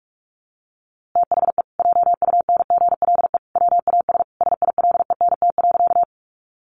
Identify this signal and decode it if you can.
Morse code, heard as a keyed tone, about 44 wpm, 700 Hz. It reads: THE JUNGLE WAS SILENT.
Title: A Sound Of Thunder 45wpm